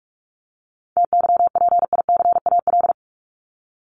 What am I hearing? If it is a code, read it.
Morse code, heard as a keyed tone, about 44 wpm, 700 Hz. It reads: TYPICAL
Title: Top 1001-1250 Words 45wpm